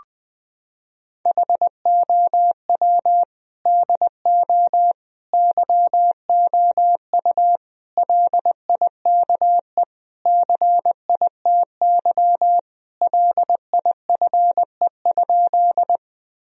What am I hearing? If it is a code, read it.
Morse code, heard as a keyed tone, about 20 wpm, 700 Hz. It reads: HOW DO YOU LIKE CITY LIFE?
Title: Sentences from Top 1000 Words - Review Part 1 - 20wpm